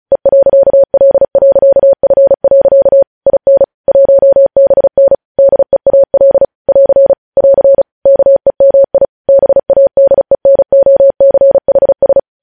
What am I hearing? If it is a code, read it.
Morse code, heard as a keyed tone, about 36 wpm, 550 Hz. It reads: E.L.F. IN 1BN DEAL <AR> <AR> KEMI BADENOCHS
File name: podcast / News headlines-1x-35x35